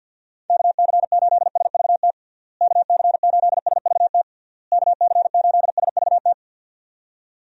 Morse code, heard as a keyed tone, about 50 wpm, 700 Hz. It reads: KC8SVT KC8SVT KC8SVT
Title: USA Call Signs - 6 Characters - Repeated 3x - 50wpm